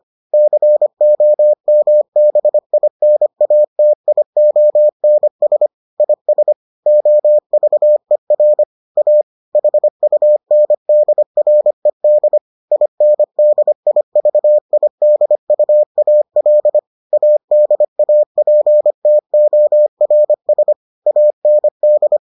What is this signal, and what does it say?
Morse code, heard as a keyed tone, about 25 wpm, 600 Hz. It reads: COMBINATIONS IS OVER A HUNDRED INDIVIDUAL ADAPTORS AND